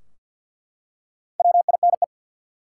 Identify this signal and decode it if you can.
Morse code, heard as a keyed tone, about 49 wpm, 700 Hz. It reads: WINE
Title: Four Letter Words - 50wpm